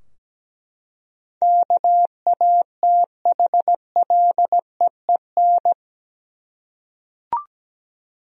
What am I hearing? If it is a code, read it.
Morse code, heard as a keyed tone, about 17 wpm, 700 Hz. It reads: KATHLEEN
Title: USA Common Names - 17wpm